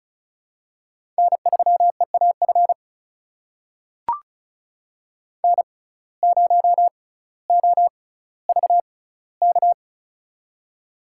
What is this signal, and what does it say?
Morse code, heard as a keyed tone, about 36 wpm, spaced as for 18 wpm, 700 Hz. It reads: N3EAF N0OVK
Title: USA Call Signs - 5 Characters - 6x Character Spacing - 35wpm